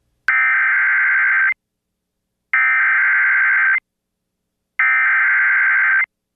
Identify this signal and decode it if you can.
Emergency Alert System (SAME) tones (header bursts), heard as harsh buzzing bursts. It is heard from 0.3 to 6.0 s.